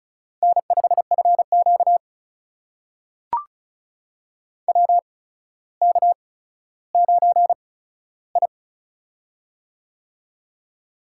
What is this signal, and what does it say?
Morse code, heard as a keyed tone, about 36 wpm, spaced as for 15 wpm, 700 Hz. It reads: N5FQ WK9I
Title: USA Call Signs - 4 Characters - 8x Character Spacing - 35wpm